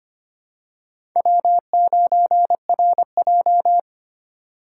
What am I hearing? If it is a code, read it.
Morse code, heard as a keyed tone, about 25 wpm, 700 Hz. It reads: W9RJ
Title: USA Call Signs - 4 Characters - 3x Character Spacing - 25wpm